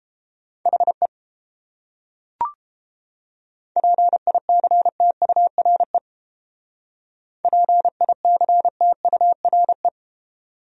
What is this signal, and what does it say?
Morse code, heard as a keyed tone, about 31 wpm, 700 Hz. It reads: HE PICTURE PICTURE